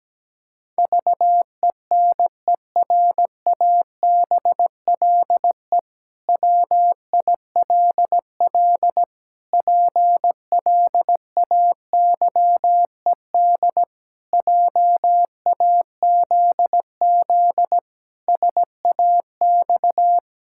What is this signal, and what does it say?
Morse code, heard as a keyed tone, about 17 wpm, 700 Hz. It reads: VENERABLE WILL PLAYED JAZZ SAX
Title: Pangram Challenge v2 - 17wpm